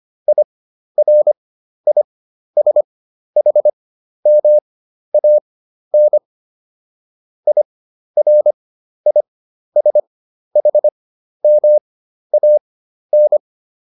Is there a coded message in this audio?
Yes: Morse code from 0.3 to 13.4 s, heard as a keyed tone, about 25 wpm, spaced as for 12 wpm, 600 Hz. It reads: IRISHMAN IRISHMAN